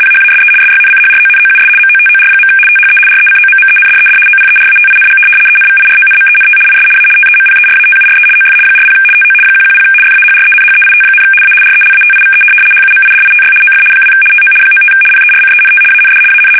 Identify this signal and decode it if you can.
RTTY, about 75 baud. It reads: PHASE SYNCHRONOUS MOTOR. THE RECOMMENDED OPERATING ENVIRONMENT IS A TEMPERATURE OF 40 TO 110 FAHRENHEIT, A RELATIVE HUMIDITY OF 2 TO 95 PERCENT AND AN
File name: fsk850-75B / teletype-corp